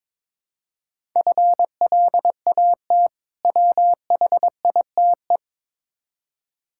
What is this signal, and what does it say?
Morse code, heard as a keyed tone, about 22 wpm, 700 Hz. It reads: FLAT WHITE
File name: Difficult Words with Hints - 22wpm